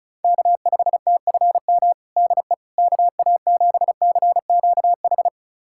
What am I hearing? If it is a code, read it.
Morse code, heard as a keyed tone, about 36 wpm, 700 Hz. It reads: K5TFM DE KA7CQH
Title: USA Callsign DE Callsign 35wpm